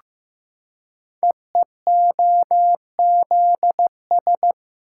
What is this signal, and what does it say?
Morse code, heard as a keyed tone, about 15 wpm, 700 Hz. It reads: EEOZS
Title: Code Groups - 5 Letters - 15wpm